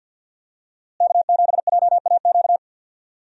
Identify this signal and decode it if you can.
Morse code, heard as a keyed tone, about 50 wpm, 700 Hz. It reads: K7JAQ
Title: USA Call Signs - 5 Characters - Repeated 2x - 50wpm